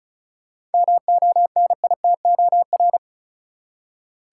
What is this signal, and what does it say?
Morse code, heard as a keyed tone, about 36 wpm, 700 Hz. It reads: MONITOR
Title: Seven Letter Words - 35wpm